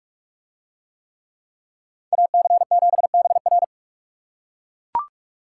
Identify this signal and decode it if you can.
Morse code, heard as a keyed tone, about 44 wpm, 700 Hz. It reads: AC7BR